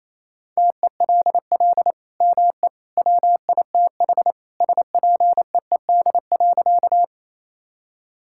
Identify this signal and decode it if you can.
Morse code, heard as a keyed tone, about 27 wpm, 700 Hz. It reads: TELL ME WITH SPEED.
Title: Sentences from Top 1000 Words - Review Part 5 - 28wpm